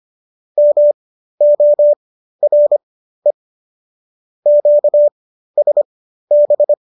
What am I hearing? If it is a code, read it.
Morse code, heard as a keyed tone, about 25 wpm, spaced as for 13 wpm, 590 Hz. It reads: MORE QSB